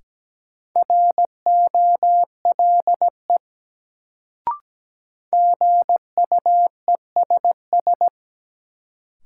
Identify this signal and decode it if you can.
Morse code, heard as a keyed tone, about 17 wpm, 700 Hz. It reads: ROLE GUESS
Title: Sentences from Top 1500 Words - New Word Drill Alt - 17wpm